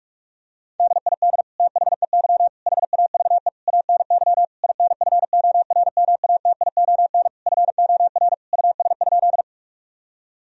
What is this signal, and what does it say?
Morse code, heard as a keyed tone, about 46 wpm, 700 Hz. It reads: DID THEY HAVE ANY INFORMATION FOR US?
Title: Sentences from Top 1000 Words - Review Part 3 - 45wpm